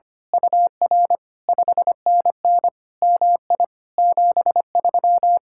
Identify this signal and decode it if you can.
Morse code, heard as a keyed tone, about 24 wpm, 700 Hz. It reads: UR 5NN MI 73
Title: Speed-Racing Common QSO Phrases - 25wpm